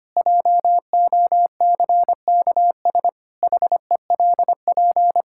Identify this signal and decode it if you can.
Morse code, heard as a keyed tone, about 25 wpm, 700 Hz. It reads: JOCKS HELP
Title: Pangram Challenge v2 - 25wpm